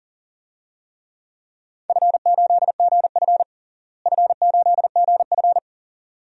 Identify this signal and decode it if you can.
Morse code, heard as a keyed tone, about 40 wpm, 700 Hz. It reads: F8GF F8GF
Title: DXCC-100 Call Signs - 4 Characters - Repeated 2x - 40wpm